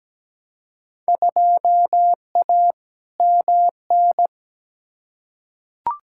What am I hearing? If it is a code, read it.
Morse code, heard as a keyed tone, about 17 wpm, 700 Hz. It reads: 2A MN
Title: Field Day Exchanges - 17wpm